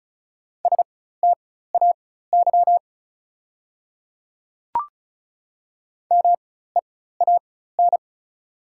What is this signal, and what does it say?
Morse code, heard as a keyed tone, about 36 wpm, spaced as for 16 wpm, 690 Hz. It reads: STAY MEAN
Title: 4x Character Spacing - Top 500 Words - 35wpm